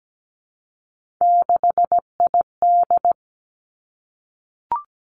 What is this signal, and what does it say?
Morse code, heard as a keyed tone, about 17 wpm, 700 Hz. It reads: 6ID
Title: Three Challenging Letter-Number Combinations 17wpm